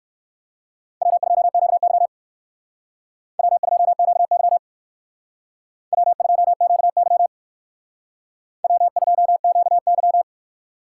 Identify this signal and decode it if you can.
Morse code, heard as a keyed tone, about 57 wpm, 700 Hz. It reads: W2QY W2QY W2QY W2QY